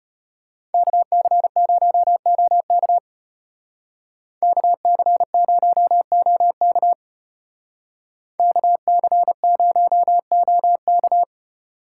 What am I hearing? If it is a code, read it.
Morse code, heard as a keyed tone, about 29 wpm, 700 Hz. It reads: KC0OK KC0OK KC0OK
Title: Speed Racing - USA 2x2 Call Signs 25wpm